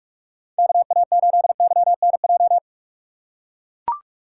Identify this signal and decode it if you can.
Morse code, heard as a keyed tone, about 43 wpm, 700 Hz. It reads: KA8YNJ
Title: USA Call Signs - 6 Characters - 6x Character Spacing - 45wpm